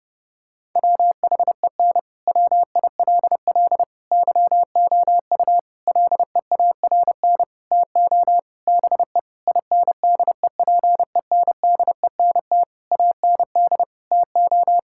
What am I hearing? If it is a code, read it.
Morse code, heard as a keyed tone, about 30 wpm, 700 Hz. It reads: WHEN WILL YOU LEARN TO BE INDEPENDENT AND TO
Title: Sentences from Top 1000 Words - Review Part 6 - 30wpm